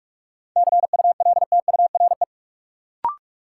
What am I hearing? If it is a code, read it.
Morse code, heard as a keyed tone, about 45 wpm, 700 Hz. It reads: CULTURE
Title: Sentences from Top 1000 Words - Review Part 6 - New Word Drill - 45wpm